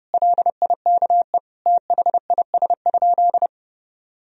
Morse code, heard as a keyed tone, about 30 wpm, 700 Hz. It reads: LIKE THIS?
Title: Sentences from Top 1000 Words - Review Part 2 - 30wpm